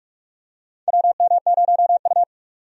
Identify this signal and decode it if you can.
Morse code, heard as a keyed tone, about 46 wpm, 700 Hz. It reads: WM0U
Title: USA Call Signs - 4 Characters - 45wpm